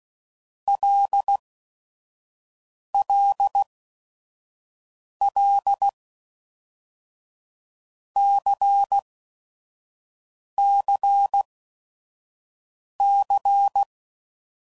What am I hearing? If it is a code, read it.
Morse code, heard as a keyed tone, about 16 wpm, 800 Hz. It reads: L L L C C C
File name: Basic Letter Groups By7Groups / LFCYQ / LFCYQ@5 f15